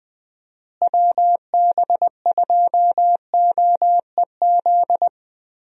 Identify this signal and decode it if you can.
Morse code, heard as a keyed tone, about 20 wpm, 700 Hz. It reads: WB2OEZ
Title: USA Call Signs - 6 Characters - Repeated 2x - 20wpm